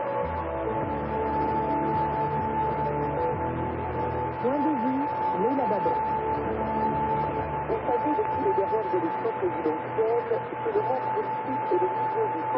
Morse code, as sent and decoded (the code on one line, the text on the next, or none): none